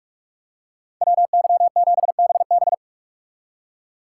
Morse code, heard as a keyed tone, about 40 wpm, 700 Hz. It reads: WY7BB